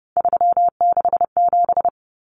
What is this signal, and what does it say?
Morse code, heard as a keyed tone, about 30 wpm, 700 Hz. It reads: 367
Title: Common Cut Numbers - 3-Digit 30wpm